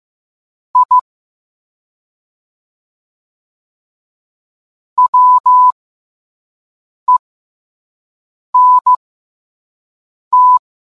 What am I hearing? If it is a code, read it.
Morse code, heard as a keyed tone, about 15 wpm, spaced as for 5 wpm, 1000 Hz. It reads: I WENT